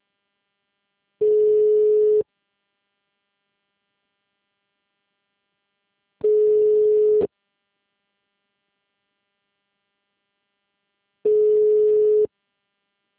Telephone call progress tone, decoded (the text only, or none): ringback tone